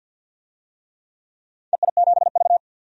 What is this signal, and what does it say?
Morse code, heard as a keyed tone, about 52 wpm, 700 Hz. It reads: EI7V
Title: DXCC-100 Call Signs - 4 Characters - Repeated 2x - 50wpm